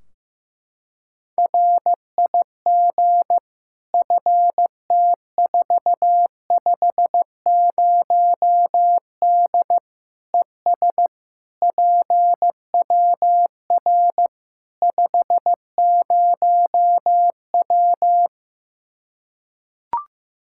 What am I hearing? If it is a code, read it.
Morse code, heard as a keyed tone, about 15 wpm, 700 Hz. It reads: RIG FT450D ES PWR 50W
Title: K1USN SST QSO Practice 15wpm